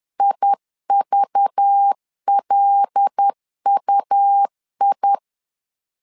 Morse code, heard as a keyed tone, about 10 wpm, 800 Hz. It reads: IVLUI
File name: LEZIONI / 5 LBVC / 5 LBV 10wpm 2248